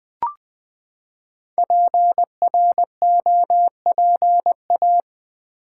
Morse code, heard as a keyed tone, about 20 wpm, 700 Hz. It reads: PROPA
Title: Build A Word - 5 Letters At A Time - 10-Letter Words - 20wpm